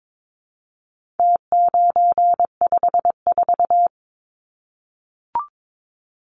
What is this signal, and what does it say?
Morse code, heard as a keyed tone, about 22 wpm, 700 Hz. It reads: T954
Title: Numbers - Contest Style - 4-Digit 22wpm